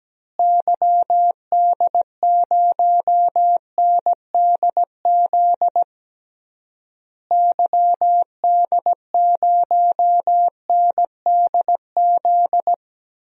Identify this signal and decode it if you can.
Morse code, heard as a keyed tone, about 17 wpm, 700 Hz. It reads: YD0NDZ YD0NDZ
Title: DXCC-100 Call Signs - 6 Characters - Repeated 3x - 17wpm